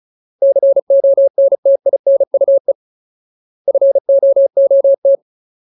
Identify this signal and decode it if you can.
Morse code, heard as a keyed tone, about 34 wpm, 550 Hz. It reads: CONTINUE FOOT
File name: Ultimate Word List / 35WPM Ultimate